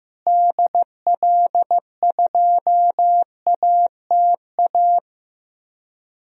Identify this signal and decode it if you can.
Morse code, heard as a keyed tone, about 15 wpm, 700 Hz. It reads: DL2ATA
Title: DXCC-100 Call Signs - 6 Characters - Repeated 2x - 15wpm